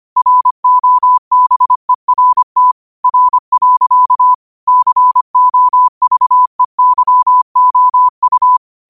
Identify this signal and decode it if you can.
Morse code, heard as a keyed tone, about 25 wpm, 1000 Hz. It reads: ROBERT R. COVEYOU
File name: qotdfiles / 25wpm